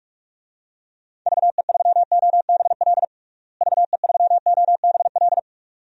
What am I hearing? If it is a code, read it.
Morse code, heard as a keyed tone, about 45 wpm, 700 Hz. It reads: VE3OBL VE3OBL
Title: DXCC-100 Call Signs - 6 Characters - Repeated 2x - 45wpm